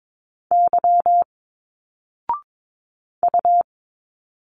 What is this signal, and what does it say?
Morse code, heard as a keyed tone, about 22 wpm, 700 Hz. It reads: Y U
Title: Single Letters 22wpm